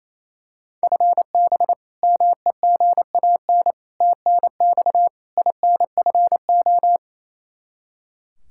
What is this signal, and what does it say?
Morse code, heard as a keyed tone, about 28 wpm, 700 Hz. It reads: FB MEGAN TNX INFO
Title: Practice QSO - Exchange 3 - More Info - 28wpm